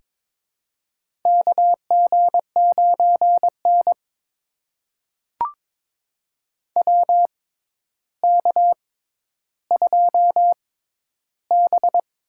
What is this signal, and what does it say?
Morse code, heard as a keyed tone, about 23 wpm, spaced as for 11 wpm, 700 Hz. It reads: KG9N WK2B
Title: USA Call Signs - 4 Characters - 6x Character Spacing - 22wpm